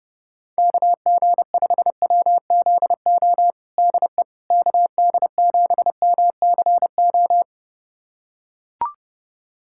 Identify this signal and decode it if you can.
Morse code, heard as a keyed tone, about 32 wpm, 700 Hz. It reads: KG5WZO DE KD7MCO
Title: USA Callsign DE Callsign 30wpm